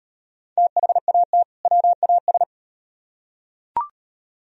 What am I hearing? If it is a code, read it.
Morse code, heard as a keyed tone, about 34 wpm, 700 Hz. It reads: THAT WAS